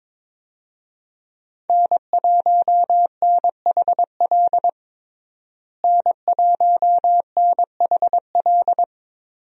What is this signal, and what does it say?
Morse code, heard as a keyed tone, about 21 wpm, 700 Hz. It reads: N1NHL N1NHL
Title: USA Call Signs - 5 Characters - Repeated 2x - 22wpm